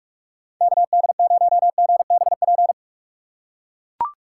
Morse code, heard as a keyed tone, about 42 wpm, 700 Hz. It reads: KD0GBP
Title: USA Call Signs - 6x Character Spacing - 45wpm